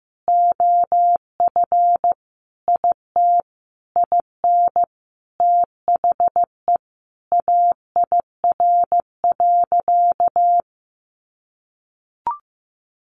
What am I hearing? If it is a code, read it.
Morse code, heard as a keyed tone, about 15 wpm, 700 Hz. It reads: OF IT IN THE AIR.
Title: Sentences from Top 900 Words 15wpm